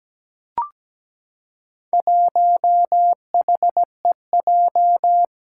Morse code, heard as a keyed tone, about 17 wpm, 700 Hz. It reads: 1HEJ
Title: Four Challenging Letter-Number Combinations 17wpm